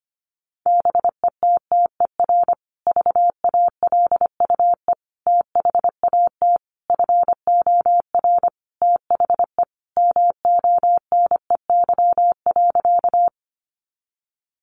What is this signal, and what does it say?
Morse code, heard as a keyed tone, about 25 wpm, 700 Hz. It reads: BETTER VALUE THAT FOR THE MONEY.
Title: Sentences from Top 600 Words 25wpm